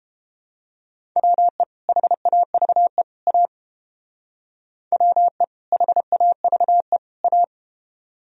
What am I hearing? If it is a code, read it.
Morse code, heard as a keyed tone, about 31 wpm, 700 Hz. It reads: WE HAVE A WE HAVE A